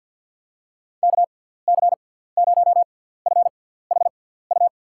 Morse code, heard as a keyed tone, about 53 wpm, spaced as for 17 wpm, 700 Hz. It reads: KC0FHU
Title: USA Call Signs - 6 Characters - 6x Character Spacing - 50wpm